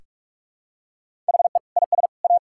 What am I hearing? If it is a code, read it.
Morse code, heard as a keyed tone, about 46 wpm, 700 Hz. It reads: HE IS A
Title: Sentences from Top 1000 Words - Review Part 5 - 45wpm